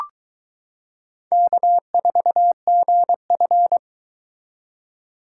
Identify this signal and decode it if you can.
Morse code, heard as a keyed tone, about 22 wpm, 700 Hz. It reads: K4GF